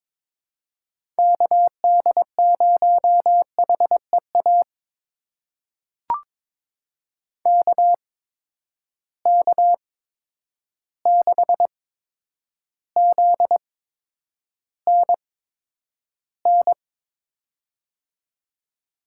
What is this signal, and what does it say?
Morse code, heard as a keyed tone, about 22 wpm, spaced as for 9 wpm, 700 Hz. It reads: KD0HEA KK6ZNN
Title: USA Call Signs - 6 Characters - 8x Character Spacing - 22wpm